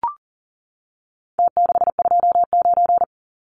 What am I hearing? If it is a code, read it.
Morse code, heard as a keyed tone, about 40 wpm, 700 Hz. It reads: T629
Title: Numbers - Contest Style - 4-Digit 40wpm